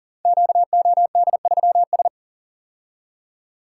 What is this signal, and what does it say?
Morse code, heard as a keyed tone, about 42 wpm, 700 Hz. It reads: QOD3S
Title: Code Groups - 5 Letter:Numbers - 40wpm